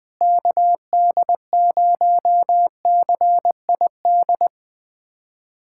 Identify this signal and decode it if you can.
Morse code, heard as a keyed tone, about 20 wpm, 700 Hz. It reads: KD0CID
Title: USA Callsign DE Callsign 20wpm